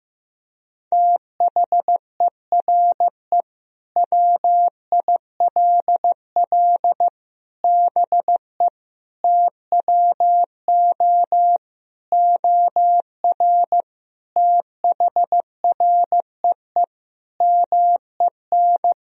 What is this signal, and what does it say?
Morse code, heard as a keyed tone, about 15 wpm, 700 Hz. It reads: THERE WILL BE TWO OR THREE MEN
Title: Sentences from Top 1000 Words - Review Part 3 - 15wpm